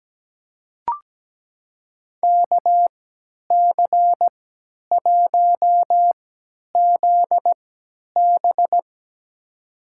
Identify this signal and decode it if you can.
Morse code, heard as a keyed tone, about 17 wpm, spaced as for 10 wpm, 700 Hz. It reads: KC1ZB
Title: USA Call Signs - 5 Characters - 3x Character Spacing - 17wpm